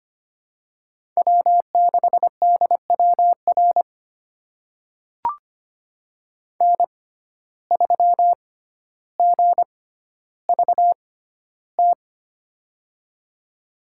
Morse code, heard as a keyed tone, about 25 wpm, spaced as for 13 wpm, 700 Hz. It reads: W6DWR N3GVT
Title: USA Call Signs - 5 Characters - 6x Character Spacing - 25wpm